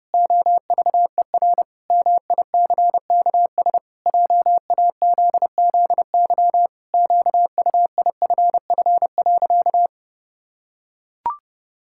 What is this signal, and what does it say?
Morse code, heard as a keyed tone, about 30 wpm, 700 Hz. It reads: OVER MICKS JAZZY QUIFF.
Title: Pangram Challenge v2 - 30wpm